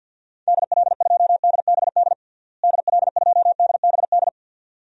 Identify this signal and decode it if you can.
Morse code, heard as a keyed tone, about 52 wpm, 700 Hz. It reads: DL2DBD DL2DBD
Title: DXCC-100 Call Signs - 6 Characters - Repeated 2x - 50wpm